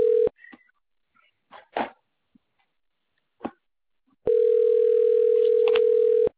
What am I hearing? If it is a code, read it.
Ringback tone (North America), 440 + 480 Hz.